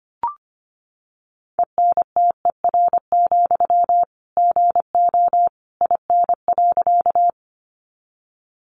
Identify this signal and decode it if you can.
Morse code, heard as a keyed tone, about 25 wpm, 700 Hz. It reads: ENTER, GO IN.
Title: Sentences from Top 1300 Words 25wpm